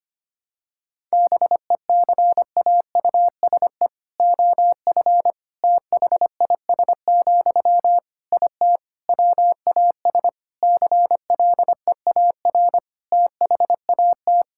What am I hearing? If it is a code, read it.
Morse code, heard as a keyed tone, about 25 wpm, 700 Hz. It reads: BECAUSE OF THIS, IT WAS CLEAR THAT
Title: Sentences from Top 1000 Words - Review Part 6 - 25wpm